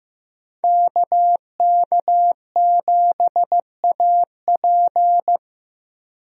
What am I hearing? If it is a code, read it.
Morse code, heard as a keyed tone, about 15 wpm, 700 Hz. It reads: KK7AP
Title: USA Call Signs - 5 Characters - Repeated 3x - 15wpm